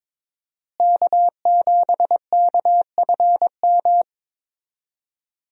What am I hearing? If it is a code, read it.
Morse code, heard as a keyed tone, about 21 wpm, 700 Hz. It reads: K7KFM
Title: USA Call Signs - 5 Characters - Repeated 2x - 22wpm